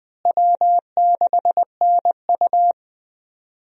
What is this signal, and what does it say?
Morse code, heard as a keyed tone, about 20 wpm, 700 Hz. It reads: W6NU
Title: USA Call Signs - 4 Characters - Repeated 3x - 20wpm